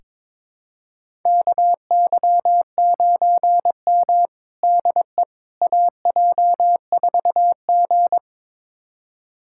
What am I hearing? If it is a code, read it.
Morse code, heard as a keyed tone, about 22 wpm, 700 Hz. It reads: KY9M DE AJ4G
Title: USA Callsign DE Callsign - 4 Characters - 22wpm